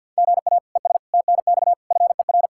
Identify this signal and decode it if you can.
Morse code, heard as a keyed tone, about 52 wpm, 700 Hz. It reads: GA ES TNX FER
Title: K1USN SST QSO Practice 50wpm